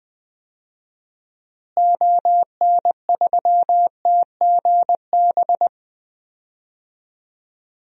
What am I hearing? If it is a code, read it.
Morse code, heard as a keyed tone, about 20 wpm, 700 Hz. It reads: ON3TGB
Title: DXCC-100 Call Signs - 6 Characters - Repeated 2x - 20wpm